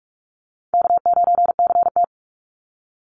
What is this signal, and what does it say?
Morse code, heard as a keyed tone, about 44 wpm, 700 Hz. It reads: K9CT
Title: Call Signs - 45wpm